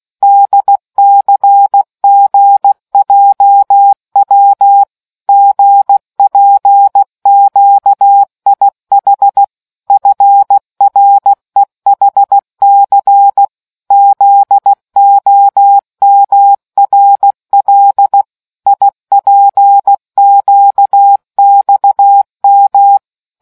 Morse code, heard as a keyed tone, about 16 wpm, 800 Hz. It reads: DCGJW GPQIH FREHC ZOMRL IPQXM